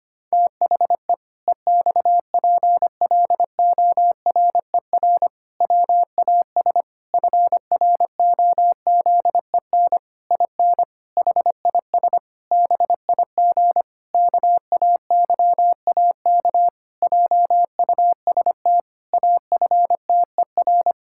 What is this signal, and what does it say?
Morse code, heard as a keyed tone, about 25 wpm, 700 Hz. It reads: THE EXPLORER WAS FROZEN IN HIS BIG KAYAK JUST AFTER
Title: Pangram Challenge v2 - 25wpm